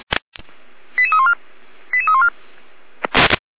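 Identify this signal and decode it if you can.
Five-tone selective call (ZVEI), heard as quick runs of five tones, tones of about 80 ms. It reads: 88314 88314
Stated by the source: ZVEI five-tone selective call 88156 received on an SDR.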